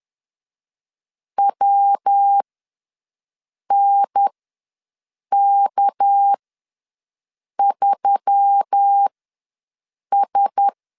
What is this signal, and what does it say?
Morse code, heard as a keyed tone, about 10 wpm, spaced as for 5 wpm, 800 Hz. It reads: WNK3S
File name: LEZIONI / 8 YJ45 / 8 YJ45 ws 10wpm 3843